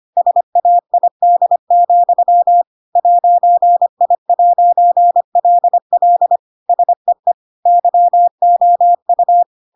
Morse code, heard as a keyed tone, about 25 wpm, 700 Hz. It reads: SAID, 'I'LL SEE YOU
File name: Pinocchio / Pinocchio ch7 25 25 ik1hge